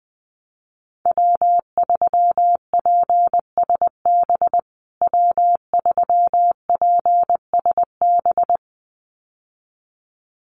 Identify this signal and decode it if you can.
Morse code, heard as a keyed tone, about 19 wpm, 700 Hz. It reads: W3PSB W3PSB
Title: USA Call Signs Repeated 2x 20wpm